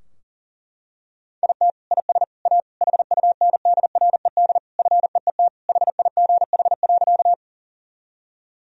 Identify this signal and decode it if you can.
Morse code, heard as a keyed tone, about 40 wpm, 700 Hz. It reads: IT IS A HUNDRED FEET HIGH.
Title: Sentences from Top 1000 Words - Review Part 2 - 40wpm